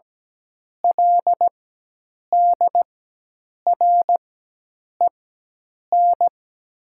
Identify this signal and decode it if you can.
Morse code, heard as a keyed tone, about 17 wpm, spaced as for 8 wpm, 700 Hz. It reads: LDREN
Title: 4x Character Spacing - Top 300 Words - 17wpm